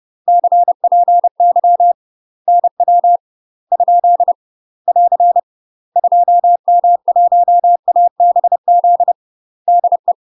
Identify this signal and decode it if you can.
Morse code, heard as a keyed tone, about 30 wpm, 700 Hz. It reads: CPY NW ? <AR> 2M1ABZ DE
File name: UK MASTER QSOs TRAINING / 30WPM / UK MASTER QSOs 0003